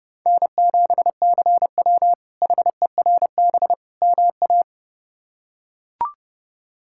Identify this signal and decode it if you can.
Morse code, heard as a keyed tone, about 32 wpm, 700 Hz. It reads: N7CW HERB MA
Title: K1USN SST QSO Practice - 30wpm